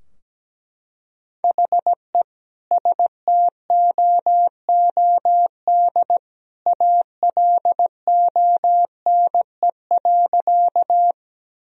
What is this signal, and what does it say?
Morse code, heard as a keyed tone, about 17 wpm, 700 Hz. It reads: HE STOOD ALONE.
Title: Sentences from Top 1000 Words - Review Part 2 - 17wpm